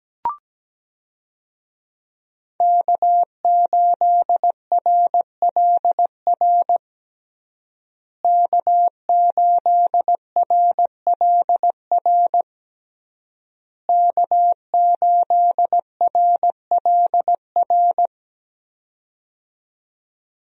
Morse code, heard as a keyed tone, about 17 wpm, 700 Hz. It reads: K8RLR K8RLR K8RLR
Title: USA Call Signs - 5 Characters - Repeated 3x - 17wpm